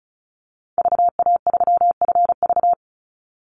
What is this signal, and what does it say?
Morse code, heard as a keyed tone, about 34 wpm, 700 Hz. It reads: VA3FV
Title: Call Signs Repeated 2x - 35wpm